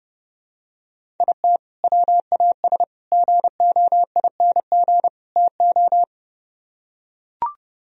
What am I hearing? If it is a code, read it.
Morse code, heard as a keyed tone, about 30 wpm, 700 Hz. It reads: IT WAS GOING TO